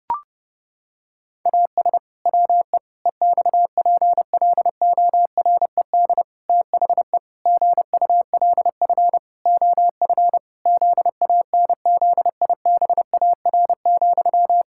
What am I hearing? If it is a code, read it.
Morse code, heard as a keyed tone, about 30 wpm, 700 Hz. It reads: AS WE EXPLORED THE GULF OF ZANZIBAR,
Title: Pangram Challenge v2 - 30wpm